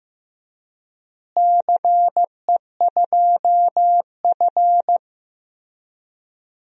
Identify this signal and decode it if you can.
Morse code, heard as a keyed tone, about 15 wpm, 700 Hz. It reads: CE2F
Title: DXCC-100 Call Signs - 4 Characters - Repeated 3x - 15wpm